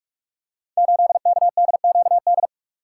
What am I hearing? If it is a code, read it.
Morse code, heard as a keyed tone, about 44 wpm, 700 Hz. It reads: 8KDQD
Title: Code Groups - 5 Letter:Numbers - 45wpm